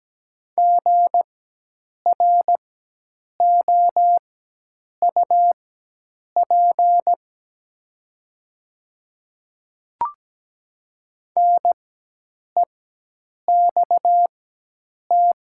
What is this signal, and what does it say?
Morse code, heard as a keyed tone, about 17 wpm, spaced as for 8 wpm, 700 Hz. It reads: GROUP NEXT
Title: 4x Character Spacing - Top 500 Words - 17wpm